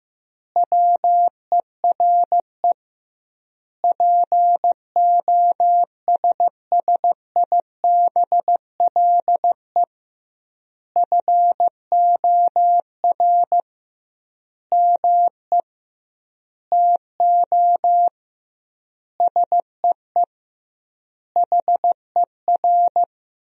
Morse code, heard as a keyed tone, about 15 wpm, 700 Hz. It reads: WERE POSSIBLE FOR ME TO SEE HER
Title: 2x Word-Spacing - Sentences from Top 500 Words - 15wpm